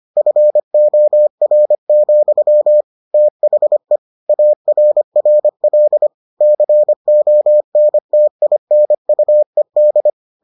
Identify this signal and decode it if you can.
Morse code, heard as a keyed tone, about 25 wpm, 600 Hz. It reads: FOR, THE ARRL CONTINUED